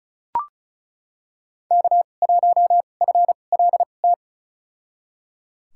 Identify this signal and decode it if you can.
Morse code, heard as a keyed tone, about 36 wpm, spaced as for 25 wpm, 700 Hz. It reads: K1FLT
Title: USA Call Signs - 5 Characters - 2x Character Spacing - 35wpm